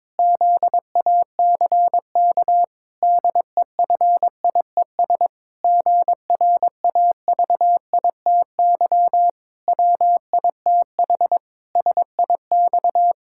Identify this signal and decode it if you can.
Morse code, heard as a keyed tone, about 22 wpm, 700 Hz. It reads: ZACK DEFIES GRAVITY WITH SIX
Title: Pangram Challenge v2 - 22wpm